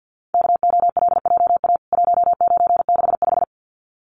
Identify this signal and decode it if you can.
Morse code, heard as a keyed tone, about 49 wpm, 700 Hz. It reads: KOLJA 1965
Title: CWT Exchange 50wpm